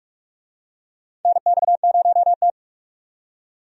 Morse code, heard as a keyed tone, about 42 wpm, 700 Hz. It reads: NX0T
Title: USA Call Signs - 4 Characters - 45wpm